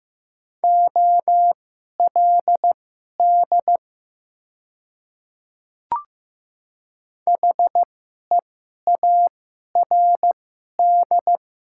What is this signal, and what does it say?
Morse code, heard as a keyed tone, about 15 wpm, spaced as for 11 wpm, 700 Hz. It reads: OLD HEARD
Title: 2x Character Spacing - Top 500 Words - 15wpm